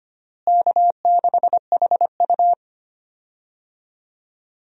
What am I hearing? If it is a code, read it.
Morse code, heard as a keyed tone, about 25 wpm, 700 Hz. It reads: K6HU
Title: USA Call Signs - 4 Characters - Repeated 3x - 25wpm